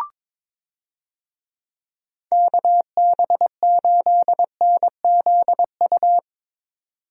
Morse code, heard as a keyed tone, about 22 wpm, 700 Hz. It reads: KB8NZU
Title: USA Call Signs - 6 Characters - 22wpm